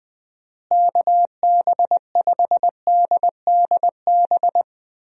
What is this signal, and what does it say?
Morse code, heard as a keyed tone, about 20 wpm, 700 Hz. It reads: KB5DDB
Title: USA Call Signs - 6 Characters - Repeated 2x - 20wpm